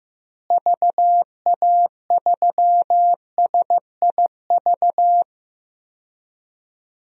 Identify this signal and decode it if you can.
Morse code, heard as a keyed tone, about 15 wpm, 700 Hz. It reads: VA3SIV
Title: DXCC-100 Call Signs - 6 Characters - Repeated 2x - 15wpm